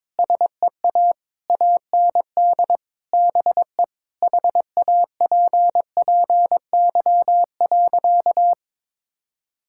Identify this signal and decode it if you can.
Morse code, heard as a keyed tone, about 22 wpm, 700 Hz. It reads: SEA AND BE HAPPY.
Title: Sentences from Top 1000 Words - Review Part 5 - 22wpm